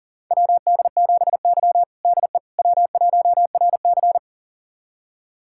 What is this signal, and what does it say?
Morse code, heard as a keyed tone, about 40 wpm, 700 Hz. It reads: WD7Y DE W1RC
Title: USA Callsign DE Callsign - 4 Characters - 40wpm